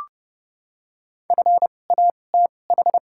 Morse code, heard as a keyed tone, about 32 wpm, spaced as for 22 wpm, 700 Hz. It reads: FATH
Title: 2x Character Spacing - Top 200 Words - 30wpm